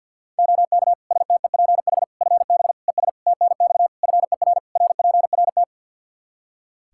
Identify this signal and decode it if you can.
Morse code, heard as a keyed tone, about 46 wpm, 700 Hz. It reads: OK STEPH FB ES TNX FER RPRT
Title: Practice QSO - Exchange 2 - Info - 50wpm